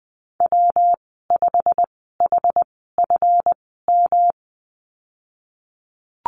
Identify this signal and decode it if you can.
Morse code, heard as a keyed tone, about 20 wpm, spaced as for 15 wpm, 700 Hz. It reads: W5HFM
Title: 140 Lesson 23g - Letter f - Call Signs 20wpm